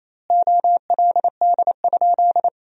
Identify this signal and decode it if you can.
Morse code, heard as a keyed tone, about 28 wpm, 700 Hz. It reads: OLD?
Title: 2x Word-Spacing - Sentences from Top 400 Words - 28wpm